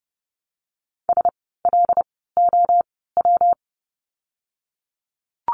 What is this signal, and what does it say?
Morse code, heard as a keyed tone, about 30 wpm, spaced as for 17 wpm, 700 Hz. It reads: SLOW
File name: course / 30:10wpm / 115 Lesson 19f - Letter w - Words 30wpm